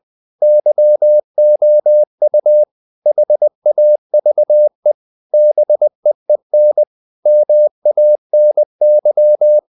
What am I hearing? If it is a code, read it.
Morse code, heard as a keyed tone, about 20 wpm, 600 Hz. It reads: YOU HAVE BEEN MANY
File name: the old man and the sea / the old man and the sea part 1 20wpm